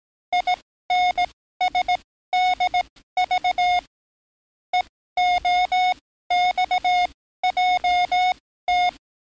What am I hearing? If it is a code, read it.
Morse code, heard as a keyed tone, about 18 wpm, spaced as for 14 wpm, 720 Hz. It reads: INSDV EOXJT
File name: training / NAVPERS 11049to11060 / NAVPERS 11056 RA RB